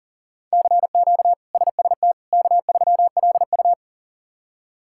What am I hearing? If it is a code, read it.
Morse code, heard as a keyed tone, about 40 wpm, 700 Hz. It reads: CQ SST K3LU
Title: K1USN SST QSO Practice - 40wpm